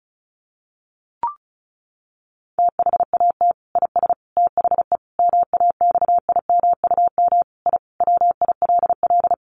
Morse code, heard as a keyed tone, about 34 wpm, 700 Hz. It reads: THAT IS THE MAXIMUM I WILL
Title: Sentences from Top 1300 Words 35wpm